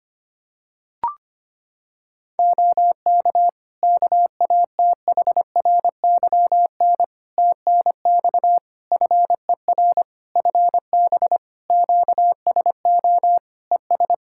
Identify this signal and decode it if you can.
Morse code, heard as a keyed tone, about 25 wpm, 700 Hz. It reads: OK KATHRYN TNX FER FB QSO ES
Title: Practice QSO - Exchange 4 - Ending - 25wpm